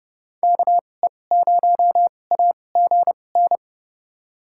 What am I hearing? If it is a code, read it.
Morse code, heard as a keyed tone, about 32 wpm, spaced as for 22 wpm, 700 Hz. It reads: KE0AGN
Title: USA Call Signs - 2x Character Spacing - 30wpm